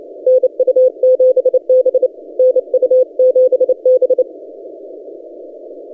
Morse code, heard as a keyed tone, about 30 wpm, 510 Hz. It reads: NU7B NU7B
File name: S9TF 20CW